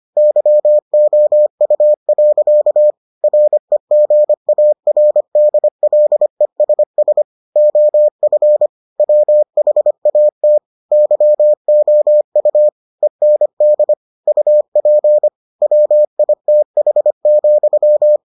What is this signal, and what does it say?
Morse code, heard as a keyed tone, about 25 wpm, 600 Hz. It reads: YOU. REGARDLESS OF WHAT YOU END UP WITH,